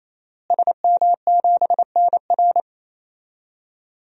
Morse code, heard as a keyed tone, about 27 wpm, 700 Hz. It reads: SM7NR
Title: DXCC-100 Call Signs - 5 Characters - Repeated 3x - 28wpm